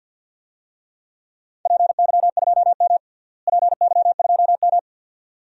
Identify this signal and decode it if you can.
Morse code, heard as a keyed tone, about 47 wpm, 700 Hz. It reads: PY2M PY2M
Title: DXCC-100 Call Signs - 4 Characters - Repeated 2x - 50wpm